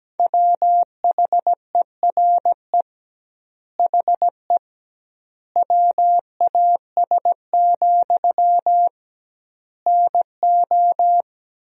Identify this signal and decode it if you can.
Morse code, heard as a keyed tone, about 17 wpm, 700 Hz. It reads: WHERE HE WAS, NO
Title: 2x Word-Spacing - Sentences from Top 400 Words - 17wpm